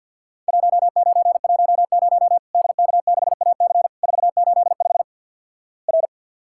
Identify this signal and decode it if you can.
Morse code, heard as a keyed tone, about 50 wpm, 700 Hz. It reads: 1910 DK6AC 485 R
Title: Worked All Europe DX Contest - QTC Exchange - 50wpm